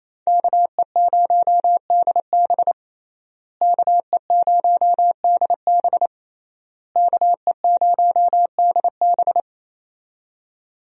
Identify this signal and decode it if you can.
Morse code, heard as a keyed tone, about 28 wpm, 700 Hz. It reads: KE0DB KE0DB KE0DB
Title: USA Call Signs - 5 Characters - Repeated 3x - 28wpm